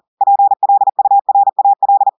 Morse code, heard as a keyed tone, about 40 wpm, 800 Hz. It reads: PLURAL
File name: audios / portugais / portugais texte en clair 1 vitesse 40 mots minutes